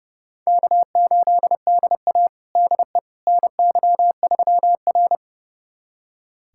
Morse code, heard as a keyed tone, about 32 wpm, 700 Hz. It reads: K8DA DE NY3R
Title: USA Callsign DE Callsign - 4 Characters - 30wpm